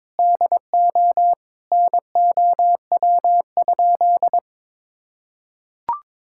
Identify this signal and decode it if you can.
Morse code, heard as a keyed tone, about 22 wpm, 700 Hz. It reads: DO NOW?
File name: practice / 22wpm / Sentences with 2-Word Phrases - 22wpm